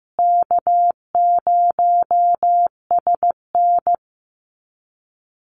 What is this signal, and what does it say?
Morse code, heard as a keyed tone, about 15 wpm, 700 Hz. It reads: K0SN
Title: Call Signs Repeated 3x - 15wpm